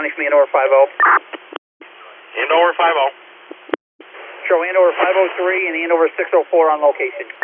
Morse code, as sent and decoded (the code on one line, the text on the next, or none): none